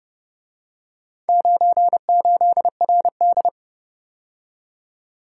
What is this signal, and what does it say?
Morse code, heard as a keyed tone, about 30 wpm, 700 Hz. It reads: 98RD